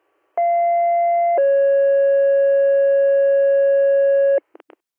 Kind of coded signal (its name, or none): two-tone page